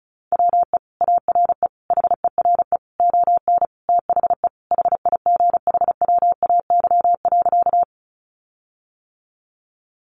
Morse code, heard as a keyed tone, about 34 wpm, 700 Hz. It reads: WE ARE HERE ON THE HIGHWAY.
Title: Sentences from Top 1400 Words 35wpm